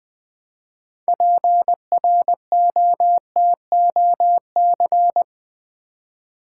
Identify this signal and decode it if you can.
Morse code, heard as a keyed tone, about 20 wpm, 700 Hz. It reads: PROTOC
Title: Build A Word - 3 Letters At A Time - 8-Letter Words - 20wpm